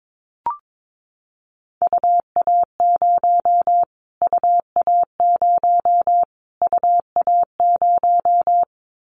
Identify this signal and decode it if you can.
Morse code, heard as a keyed tone, about 22 wpm, 700 Hz. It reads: UA0 UA0 UA0
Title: Callsign Prefixes for Top 25 DXCC - Repeated 3x - 22wpm